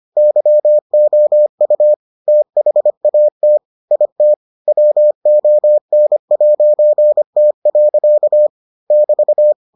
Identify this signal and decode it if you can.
Morse code, heard as a keyed tone, about 25 wpm, 600 Hz. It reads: YOU THAT IT WON'T. <BT>